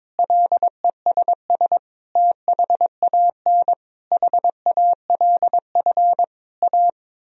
Morse code, heard as a keyed tone, about 22 wpm, 700 Hz. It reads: LESS THAN HALF A
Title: Sentences from Top 1000 Words - Review Part 2 - 22wpm